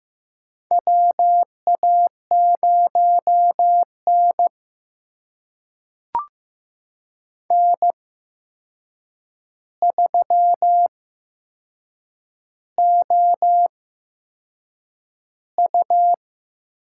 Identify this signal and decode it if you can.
Morse code, heard as a keyed tone, about 15 wpm, spaced as for 6 wpm, 700 Hz. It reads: WA0N N3OU
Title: USA Call Signs - 4 Characters - 8x Character Spacing - 15wpm